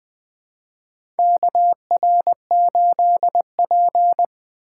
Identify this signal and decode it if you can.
Morse code, heard as a keyed tone, about 20 wpm, 700 Hz. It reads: KR8P
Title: 2023 POTA Hunter Call Signs - 20wpm